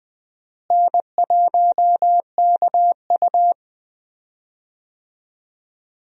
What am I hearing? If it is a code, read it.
Morse code, heard as a keyed tone, about 20 wpm, 700 Hz. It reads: N1KU
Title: USA Call Signs - 4 Characters - Repeated 2x - 20wpm